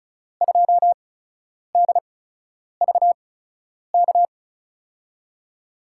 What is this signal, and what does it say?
Morse code, heard as a keyed tone, about 36 wpm, spaced as for 10 wpm, 700 Hz. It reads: 2DVK
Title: USA Call Signs - 6 Characters - 8x Character Spacing - 35wpm